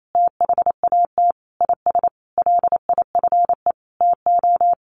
Morse code, heard as a keyed tone, about 28 wpm, 700 Hz. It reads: THAT IS LIFE TO
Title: Sentences from Top 500 Words 28wpm